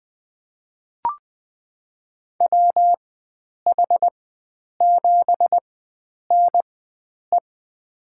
Morse code, heard as a keyed tone, about 21 wpm, spaced as for 9 wpm, 700 Hz. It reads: WH7NE
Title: USA Call Signs - 4x Character Spacing - 20wpm